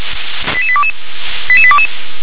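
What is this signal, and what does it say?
Five-tone selective call (ZVEI), heard as quick runs of five tones, tones of about 70 ms. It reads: 88822 88822
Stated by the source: ZVEI five-tone call received on SDR.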